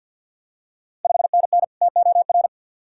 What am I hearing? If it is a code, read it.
Morse code, heard as a keyed tone, about 53 wpm, 700 Hz. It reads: 5NN TOR
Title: Ontario QSO Party Exchange 50wpm